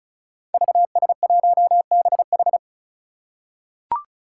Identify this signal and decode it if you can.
Morse code, heard as a keyed tone, about 36 wpm, 700 Hz. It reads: VS1BH
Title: Five Challenging Letter-Number Combinations 35wpm